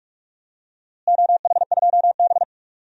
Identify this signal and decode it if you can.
Morse code, heard as a keyed tone, about 46 wpm, 700 Hz. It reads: OH2B
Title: DXCC-100 Call Signs - 4 Characters - Repeated 3x - 45wpm